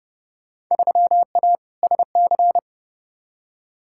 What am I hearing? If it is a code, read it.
Morse code, heard as a keyed tone, about 30 wpm, 700 Hz. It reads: 3A SC
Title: Field Day Exchanges - 30wpm